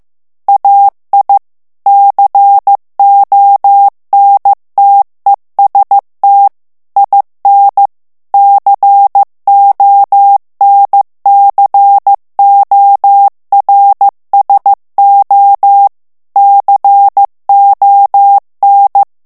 Morse code, heard as a keyed tone, about 15 wpm, 800 Hz. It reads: AI CONTEST IN CONCORSO CON
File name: morse mp3 / ITALIEN texte clair 2 vitesse 14 mots minute